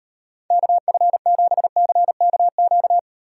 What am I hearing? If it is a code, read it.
Morse code, heard as a keyed tone, about 38 wpm, 700 Hz. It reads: KF7CKQ